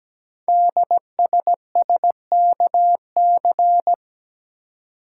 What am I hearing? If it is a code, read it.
Morse code, heard as a keyed tone, about 17 wpm, 700 Hz. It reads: DSSKC
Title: Code Groups - 5 Letters - 17wpm